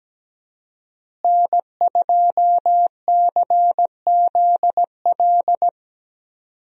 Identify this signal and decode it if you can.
Morse code, heard as a keyed tone, about 17 wpm, 700 Hz. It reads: N2CZL
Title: USA Call Signs - 5 Characters - Repeated 2x - 17wpm